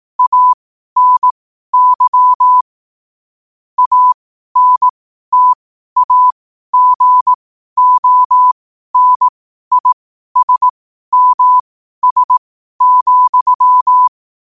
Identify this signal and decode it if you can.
Morse code, heard as a keyed tone, about 18 wpm, spaced as for 12 wpm, 1000 Hz. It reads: ANY ANTAGONISMS,
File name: qotdfiles / 12wpm